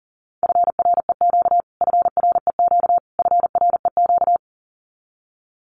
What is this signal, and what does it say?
Morse code, heard as a keyed tone, about 40 wpm, 700 Hz. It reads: FREQ FREQ FREQ
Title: Abbreviations Repeated 3x - 40wpm